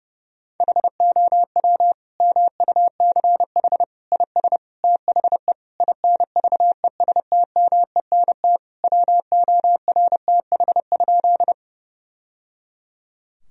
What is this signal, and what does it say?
Morse code, heard as a keyed tone, about 32 wpm, 700 Hz. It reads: HOW MUCH IS THE INVESTMENT WORTH?
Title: Sentences from Top 1000 Words - Review Part 4 - 30wpm